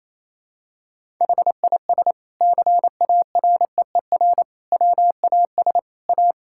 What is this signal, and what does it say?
Morse code, heard as a keyed tone, about 28 wpm, 700 Hz. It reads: HIS CAREER WAS A
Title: Sentences from Top 1000 Words - Review Part 6 - 28wpm